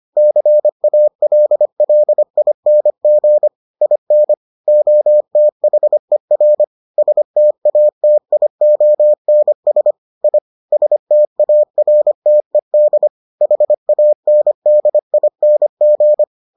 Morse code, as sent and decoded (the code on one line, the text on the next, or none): -.-. .- .-.. .-.. .. -. --. / .. -. / --- - .... .-. / ... - .- - .. --- -. ... / .. / ... - .- .-. - . -.. / .... .- -. -.. .. -. --.
CALLING IN OTHER STATIONS I STARTED HANDING